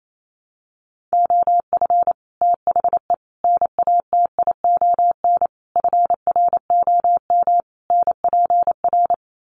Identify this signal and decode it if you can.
Morse code, heard as a keyed tone, about 28 wpm, 700 Hz. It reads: OF THE NATION FROM NPR
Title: Sets of 5 Words 28wpm